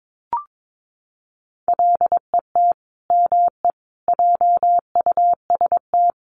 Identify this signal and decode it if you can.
Morse code, heard as a keyed tone, about 22 wpm, 700 Hz. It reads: LET ME JUST
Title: Sets of 5 Words Encore2 22wpm